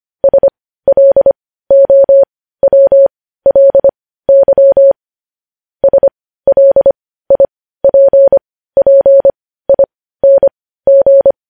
Morse code, heard as a keyed tone, about 25 wpm, spaced as for 15 wpm, 550 Hz. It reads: SLOWLY SLIPPING
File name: podcast / News headlines-1x-25x15